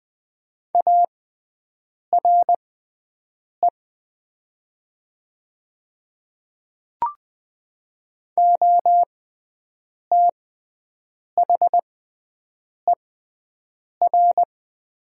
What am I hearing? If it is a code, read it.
Morse code, heard as a keyed tone, about 20 wpm, spaced as for 7 wpm, 700 Hz. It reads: ARE OTHER
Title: 6x Character Spacing - Top 100 Words - 20wpm